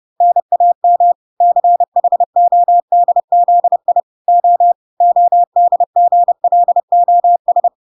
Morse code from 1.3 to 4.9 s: CHODZI O